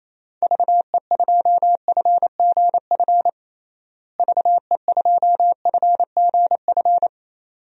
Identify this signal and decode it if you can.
Morse code, heard as a keyed tone, about 28 wpm, 700 Hz. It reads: VE2FGF VE2FGF
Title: DXCC-100 Call Signs - 6 Characters - Repeated 2x - 28wpm